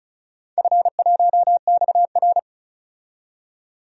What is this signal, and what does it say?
Morse code, heard as a keyed tone, about 34 wpm, 700 Hz. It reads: F1XR
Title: DXCC-100 Call Signs - 4 Characters - 35wpm